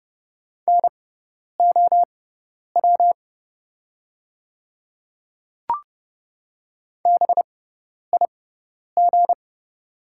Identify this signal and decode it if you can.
Morse code, heard as a keyed tone, about 30 wpm, spaced as for 10 wpm, 690 Hz. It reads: NOW BIG
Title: 6x Character Spacing - Top 200 Words - 30wpm